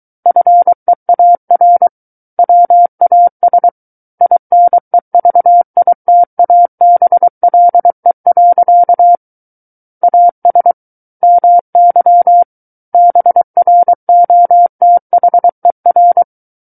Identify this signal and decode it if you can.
Morse code, heard as a keyed tone, about 23 wpm, 700 Hz. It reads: FEAR WAS INEVITABLE. AS MY BROTHER
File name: Book One / 14-Chapter 14 - 23 WPM